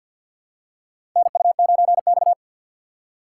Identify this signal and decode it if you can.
Morse code, heard as a keyed tone, about 52 wpm, 700 Hz. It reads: NU9X
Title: USA Call Signs - 4 Characters - 6x Character Spacing - 50wpm